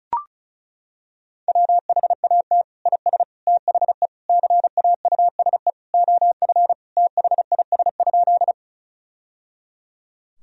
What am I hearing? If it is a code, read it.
Morse code, heard as a keyed tone, about 34 wpm, 700 Hz. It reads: WHAT IS THE CAUSE OF THIS?
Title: Sentences from Top 1000 Words - Review Part 1 - 35wpm